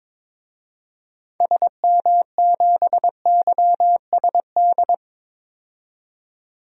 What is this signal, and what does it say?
Morse code, heard as a keyed tone, about 22 wpm, 700 Hz. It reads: SM7YSD
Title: DXCC-100 Call Signs - 6 Characters - 22wpm